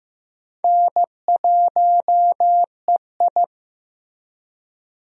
Morse code, heard as a keyed tone, about 15 wpm, 700 Hz. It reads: N1EI
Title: USA Call Signs - 4 Characters - Repeated 2x - 15wpm